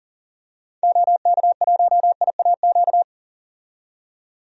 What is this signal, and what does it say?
Morse code, heard as a keyed tone, about 39 wpm, 700 Hz. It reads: OK1IAQ
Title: DXCC-100 Call Signs - 6 Characters - 40wpm